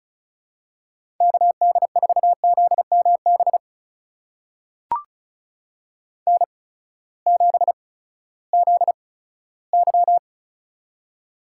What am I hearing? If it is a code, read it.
Morse code, heard as a keyed tone, about 34 wpm, spaced as for 17 wpm, 700 Hz. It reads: KD4ZMB N7ZY
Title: USA Call Signs - 8x Character Spacing - 35wpm